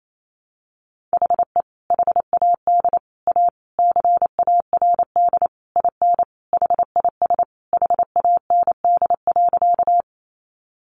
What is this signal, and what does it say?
Morse code, heard as a keyed tone, about 28 wpm, 700 Hz. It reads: HE HAD A CARD IN HIS HAND.
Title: Sentences from Top 600 Words 28wpm